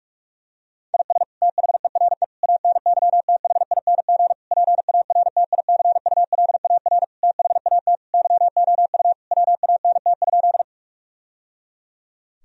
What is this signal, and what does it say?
Morse code, heard as a keyed tone, about 46 wpm, 700 Hz. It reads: IS THERE ANYTHING PARTICULAR THAT YOU WANT?
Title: Sentences from Top 1000 Words - Review Part 4 - 45wpm